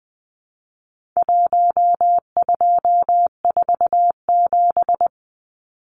Morse code, heard as a keyed tone, about 20 wpm, 700 Hz. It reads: 1247
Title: Numbers - 4-Digit 20wpm